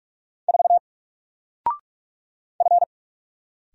Morse code, heard as a keyed tone, about 44 wpm, 700 Hz. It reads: 4 F
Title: Single Letter-Number 45wpm